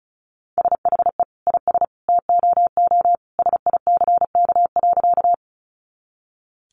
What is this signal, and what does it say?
Morse code, heard as a keyed tone, about 34 wpm, 700 Hz. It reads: SHE IS TOO SICK.
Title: Easy Phrases 35wpm